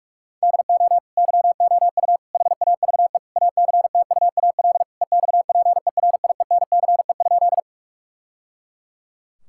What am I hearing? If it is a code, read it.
Morse code, heard as a keyed tone, about 47 wpm, 700 Hz. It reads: DO YOU HAVE ACTUAL EXPERIENCE?
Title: Sentences from Top 1000 Words - Review Part 5 - 45wpm